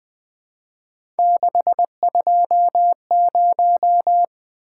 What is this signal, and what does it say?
Morse code, heard as a keyed tone, about 20 wpm, 700 Hz. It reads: 620
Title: Three Letter-Number Combinations - 20wpm